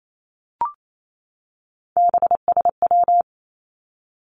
Morse code, heard as a keyed tone, about 28 wpm, 700 Hz. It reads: BSW
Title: Three Challenging Letter-Number Combinations 28wpm